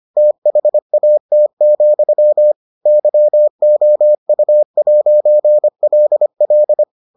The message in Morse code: THAT, YOU'LL

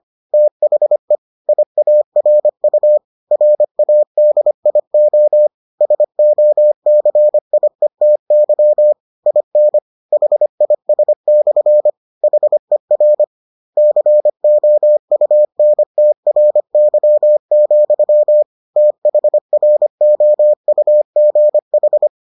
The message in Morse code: THE IARU RADIO SOCIETY IN HIS/ HER COUNTRY, THROUGH

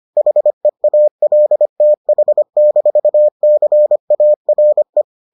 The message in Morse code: HEALTH-CARE

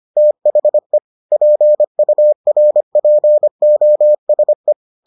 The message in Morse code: THE PURPOSE

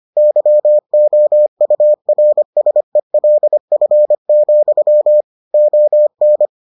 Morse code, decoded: YOURSELF, ON

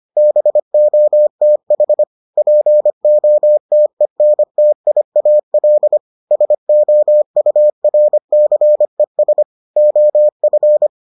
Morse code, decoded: BOTH POTENTIAL SOURCES OF

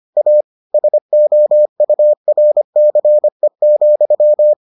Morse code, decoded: A SOURCE,

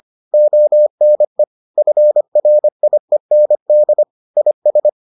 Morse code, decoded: ONE FRIEND IS

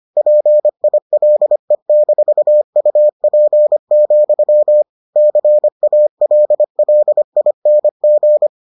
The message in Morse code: PILE-UP, CALLING

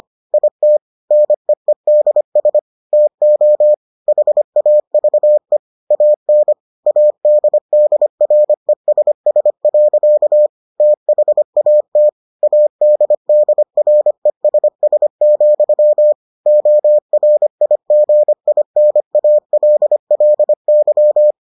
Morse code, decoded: IT NEEDS TO HAVE AN ADDRESS. THAT ADDRESS, ORIGINALLY

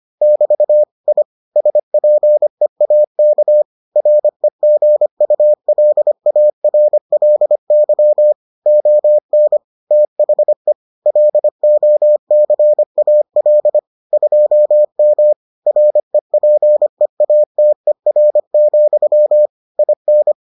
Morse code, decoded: <BT> I SPEAK REGULARLY ON THE LOCAL 2M REPEATER, IN

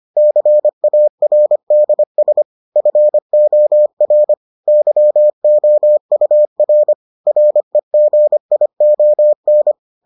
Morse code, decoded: CARDS FOR YOUR REGION

